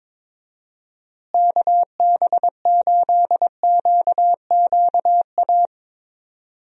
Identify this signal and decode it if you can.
Morse code, heard as a keyed tone, about 21 wpm, 700 Hz. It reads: KB8QQA
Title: USA Call Signs - 6 Characters - Repeated 2x - 22wpm